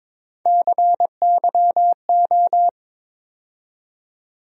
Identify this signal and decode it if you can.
Morse code, heard as a keyed tone, about 22 wpm, 700 Hz. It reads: CYO
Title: USA Call Sign Postfixes - 3 Letters 22wpm